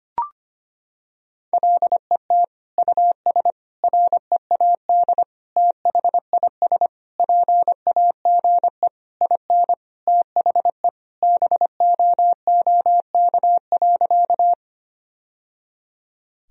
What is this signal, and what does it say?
Morse code, heard as a keyed tone, about 25 wpm, 700 Hz. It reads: LET US READ THIS PAGE IN THE BOOK.
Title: Sentences from Top 1000 Words - Review Part 1 - 25wpm